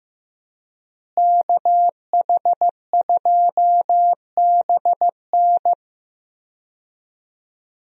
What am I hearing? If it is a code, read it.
Morse code, heard as a keyed tone, about 15 wpm, 700 Hz. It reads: KH2BN
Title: USA Call Signs - 5 Characters - Repeated 2x - 15wpm